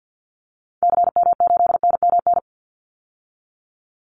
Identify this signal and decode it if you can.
Morse code, heard as a keyed tone, about 48 wpm, 700 Hz. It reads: CM8NMN
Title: Call Signs Repeated 3x - 50wpm